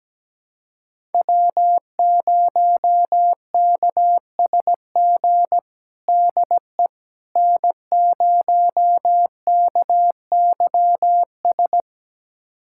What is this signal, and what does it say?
Morse code, heard as a keyed tone, about 17 wpm, 700 Hz. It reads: W0KSG DE N0KYS
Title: USA Callsign DE Callsign - 5 Characters - 17wpm